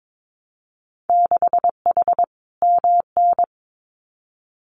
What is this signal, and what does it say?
Morse code, heard as a keyed tone, about 22 wpm, 700 Hz. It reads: 6H MN
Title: Winter Field Day Exchanges Encore 22wpm